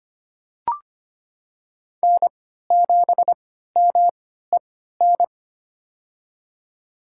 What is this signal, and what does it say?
Morse code, heard as a keyed tone, about 25 wpm, spaced as for 14 wpm, 700 Hz. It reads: N7MEN
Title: USA Call Signs - 5 Characters - 3x Character Spacing - 25wpm